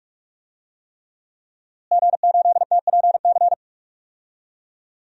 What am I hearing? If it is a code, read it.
Morse code, heard as a keyed tone, about 42 wpm, 700 Hz. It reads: G8TPC
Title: DXCC-100 Call Signs - 5 Characters - Repeated 3x - 45wpm